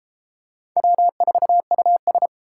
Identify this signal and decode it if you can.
Morse code, heard as a keyed tone, about 33 wpm, 700 Hz. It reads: W4US